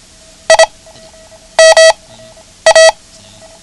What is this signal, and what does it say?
Morse code, heard as a keyed tone, about 26 wpm, 650 Hz. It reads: I M A
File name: ham radio / CodeQuick / Code Quick 09